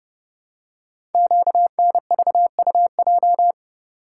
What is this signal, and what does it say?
Morse code, heard as a keyed tone, about 30 wpm, 700 Hz. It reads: QNVUJ
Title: Code Groups - 5 Letters - 30wpm